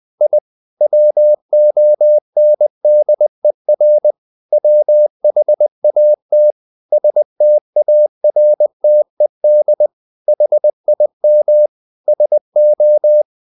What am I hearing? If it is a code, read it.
Morse code, heard as a keyed tone, about 20 wpm, 600 Hz. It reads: I WONDER WHAT STARTED HIM SO
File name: the old man and the sea part 2 20wpm